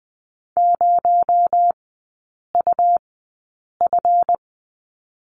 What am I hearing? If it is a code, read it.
Morse code, heard as a keyed tone, about 20 wpm, spaced as for 8 wpm, 700 Hz. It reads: 0UF
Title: 218 Lesson 35e - Number 0 - Three Characters 20wpm